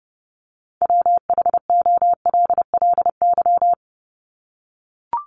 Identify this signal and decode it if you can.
Morse code, heard as a keyed tone, about 30 wpm, 700 Hz. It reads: WHOLLY
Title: LY Suffix - Top 100 Words with Base Words 30wpm